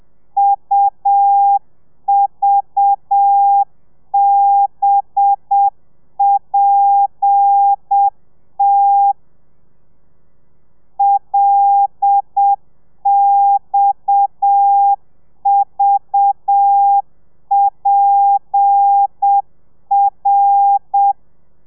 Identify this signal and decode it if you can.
Morse code, heard as a keyed tone, about 7 wpm, 800 Hz. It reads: UVBPT LXVPR
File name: telegrafia / i2tfj / Lez7 30